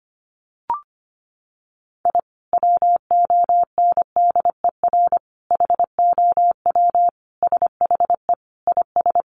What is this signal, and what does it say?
Morse code, heard as a keyed tone, about 25 wpm, 700 Hz. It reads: I WONDER HOW SHE IS
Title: Sentences from Top 1500 Words 25wpm